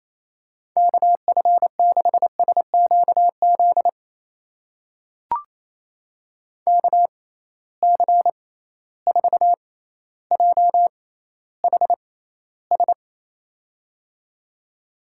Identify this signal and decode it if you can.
Morse code, heard as a keyed tone, about 28 wpm, spaced as for 14 wpm, 700 Hz. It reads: KF6SQZ KC4JHS
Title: USA Call Signs - 6 Characters - 6x Character Spacing - 28wpm